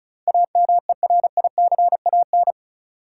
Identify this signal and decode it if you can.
Morse code, heard as a keyed tone, about 36 wpm, 700 Hz. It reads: AMERICAN
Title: Sets of 2 Words - New Word Drill - 35wpm